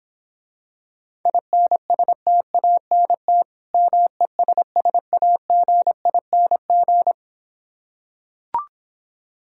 Morse code, heard as a keyed tone, about 26 wpm, 700 Hz. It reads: INSTANT MESSAGING